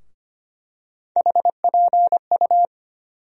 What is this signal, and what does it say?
Morse code, heard as a keyed tone, about 25 wpm, 700 Hz. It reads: HPU
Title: USA Call Sign Postfixes - 3 Letters 25wpm